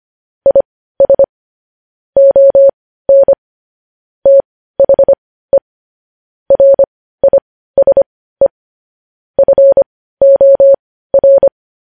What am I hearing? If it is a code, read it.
Morse code, heard as a keyed tone, about 25 wpm, spaced as for 15 wpm, 550 Hz. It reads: IS ON THE RISE FOR
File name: podcast / News headlines-1x-25x15